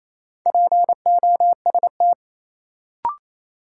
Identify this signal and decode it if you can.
Morse code, heard as a keyed tone, about 28 wpm, 700 Hz. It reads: POST
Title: Sentences from Top 1000 Words - New Word Drill - 28wpm